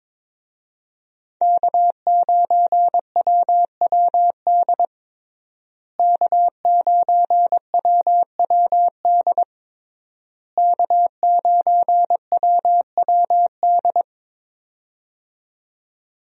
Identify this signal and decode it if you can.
Morse code, heard as a keyed tone, about 22 wpm, 700 Hz. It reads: K9WWD K9WWD K9WWD
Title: USA Call Signs - 5 Characters - Repeated 3x - 22wpm